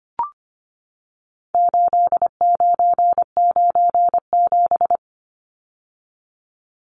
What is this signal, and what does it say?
Morse code, heard as a keyed tone, about 25 wpm, 700 Hz. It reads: 8997
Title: Numbers - 4-Digit 25wpm